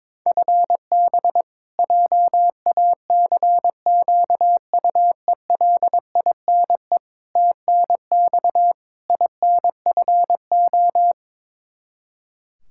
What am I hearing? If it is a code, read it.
Morse code, heard as a keyed tone, about 22 wpm, 700 Hz. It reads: FB JACQUELINE TNX INFO
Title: Practice QSO - Exchange 3 - More Info - 22wpm